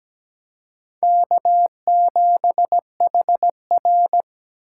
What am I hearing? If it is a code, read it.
Morse code, heard as a keyed tone, about 17 wpm, 700 Hz. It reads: K7HR
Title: USA Call Signs - 4x Character Spacing - 17wpm